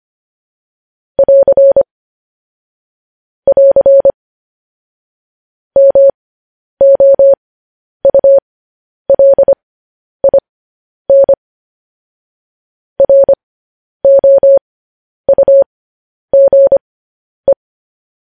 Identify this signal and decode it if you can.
Morse code, heard as a keyed tone, about 25 wpm, spaced as for 10 wpm, 550 Hz. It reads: <AR> <AR> MOULIN ROUGE
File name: podcast / News headlines-1x-25x10